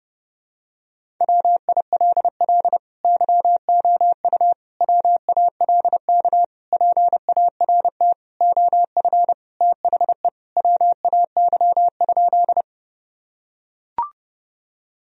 Morse code, heard as a keyed tone, about 30 wpm, 700 Hz. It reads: WILL YOU WALK PART OF THE WAY?
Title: Sentences from Top 1000 Words - Review Part 1 - 30wpm